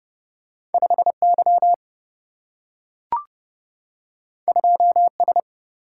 Morse code, heard as a keyed tone, about 30 wpm, 700 Hz. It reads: 5Y 2S
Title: Single-Letter Single-Number 30wpm